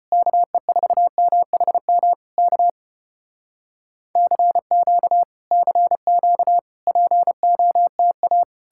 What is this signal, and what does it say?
Morse code, heard as a keyed tone, about 33 wpm, 700 Hz. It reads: KE4MHM K CQ CQ POTA